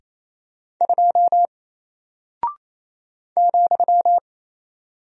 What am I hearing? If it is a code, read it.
Morse code, heard as a keyed tone, about 28 wpm, 700 Hz. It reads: 2 ,
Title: Single Character (Letter:Number:Punctuation:Prosign) Alt 28wpm